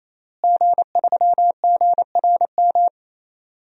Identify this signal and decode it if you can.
Morse code, heard as a keyed tone, about 28 wpm, 700 Hz. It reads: G3GRM
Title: DXCC-100 Call Signs - 5 Characters - Repeated 3x - 28wpm